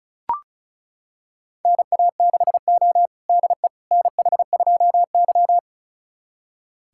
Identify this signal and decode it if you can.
Morse code, heard as a keyed tone, about 36 wpm, 700 Hz. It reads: NA6O DE NH2Y
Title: USA Callsign DE Callsign - 4 Characters - 35wpm